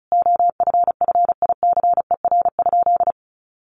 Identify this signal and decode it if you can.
Morse code, heard as a keyed tone, about 34 wpm, 700 Hz. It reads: OFFICER?
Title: Sentences from Top 1200 Words 35wpm